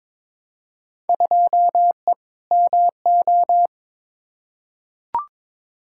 Morse code, heard as a keyed tone, about 22 wpm, 700 Hz. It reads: 2E MO
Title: Field Day Exchanges - 22wpm